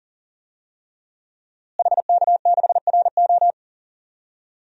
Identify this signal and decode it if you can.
Morse code, heard as a keyed tone, about 40 wpm, 700 Hz. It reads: HK6RO
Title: DXCC-100 Call Signs - 5 Characters - Repeated 3x - 40wpm